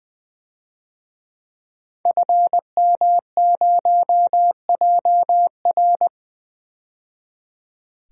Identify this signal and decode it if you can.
Morse code, heard as a keyed tone, about 20 wpm, 700 Hz. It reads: FM0JR
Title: DXCC-100 Call Signs - 5 Characters - 20wpm